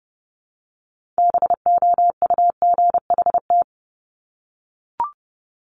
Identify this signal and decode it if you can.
Morse code, heard as a keyed tone, about 30 wpm, 700 Hz. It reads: BOUGHT
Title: Multi-letter Phonemes 30wpm